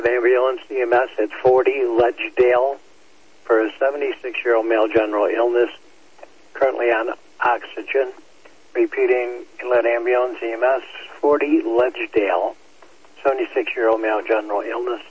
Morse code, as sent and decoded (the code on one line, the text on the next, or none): none